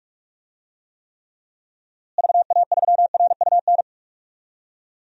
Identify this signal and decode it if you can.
Morse code, heard as a keyed tone, about 41 wpm, 700 Hz. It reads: VA3RUN
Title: DXCC-100 Call Signs - 6 Characters - Repeated 3x - 45wpm